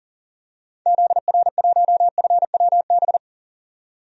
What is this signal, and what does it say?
Morse code, heard as a keyed tone, about 40 wpm, 700 Hz. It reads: ZR1FWB
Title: DXCC-100 Call Signs - 6 Characters - 40wpm